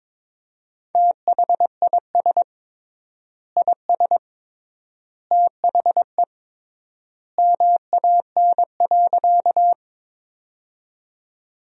Morse code, heard as a keyed tone, about 22 wpm, 700 Hz. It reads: THIS IS THE MAN.
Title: 3x Word-Spacing - Sentences from Top 200 Words - 22wpm